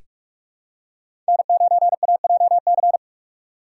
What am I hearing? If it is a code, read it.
Morse code, heard as a keyed tone, about 42 wpm, 700 Hz. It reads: N9AJC
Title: USA Call Signs - 5 Characters - 6x Character Spacing - 45wpm